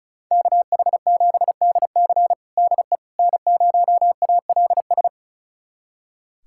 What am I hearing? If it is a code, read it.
Morse code, heard as a keyed tone, about 36 wpm, 700 Hz. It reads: KH7DC DE N0ALS
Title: USA Callsign DE Callsign - 5 Characters - 35wpm